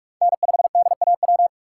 Morse code, heard as a keyed tone, about 46 wpm, 700 Hz. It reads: N5DAW
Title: USA Call Signs - 5 Characters - 2x Character Spacing - 45wpm